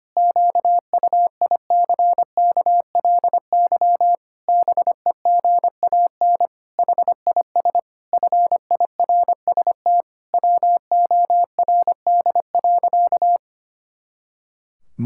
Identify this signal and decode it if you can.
Morse code, heard as a keyed tone, about 25 wpm, 700 Hz. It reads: QUICKLY BEGAN HIS FIRST WORD.
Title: Pangram Challenge v2 - 25wpm